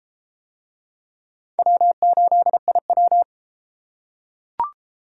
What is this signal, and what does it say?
Morse code, heard as a keyed tone, about 29 wpm, 700 Hz. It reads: W8IW